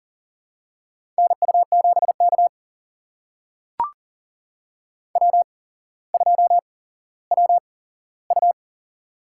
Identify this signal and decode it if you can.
Morse code, heard as a keyed tone, about 40 wpm, spaced as for 17 wpm, 700 Hz. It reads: NU7K W2WU
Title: USA Call Signs - 4 Characters - 8x Character Spacing - 40wpm